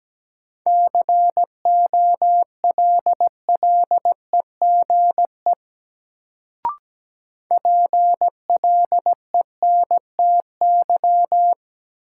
Morse code, heard as a keyed tone, about 17 wpm, 700 Hz. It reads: COLLEGE PLENTY
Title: Sentences from Top 1300 Words - New Word Drill - 17wpm